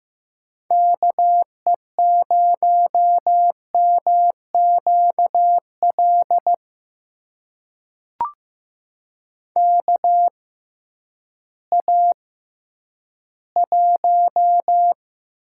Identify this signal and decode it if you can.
Morse code, heard as a keyed tone, about 15 wpm, spaced as for 10 wpm, 700 Hz. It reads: KE0MQL KA1
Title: USA Call Signs - 6 Characters - 6x Character Spacing - 15wpm